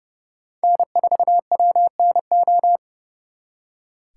Morse code, heard as a keyed tone, about 39 wpm, 700 Hz. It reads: N4WNO